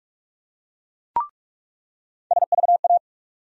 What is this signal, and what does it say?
Morse code, heard as a keyed tone, about 42 wpm, 700 Hz. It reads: SVA
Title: USA Call Sign Postfixes - 3 Letters 45wpm